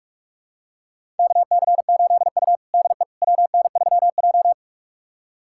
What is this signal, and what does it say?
Morse code, heard as a keyed tone, about 46 wpm, 700 Hz. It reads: KC8U DE WN3J
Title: USA Callsign DE Callsign - 4 Characters - 45wpm